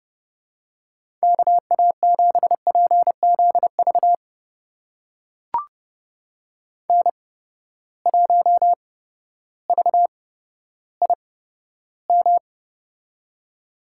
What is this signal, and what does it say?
Morse code, heard as a keyed tone, about 30 wpm, spaced as for 14 wpm, 700 Hz. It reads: KA7PZV N1VIM
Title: USA Call Signs - 8x Character Spacing - 30wpm